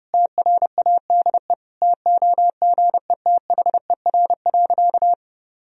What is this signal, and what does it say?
Morse code, heard as a keyed tone, about 30 wpm, 700 Hz. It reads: TRADE TOGETHER.
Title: Sentences from Top 1000 Words - Review Part 4 - 30wpm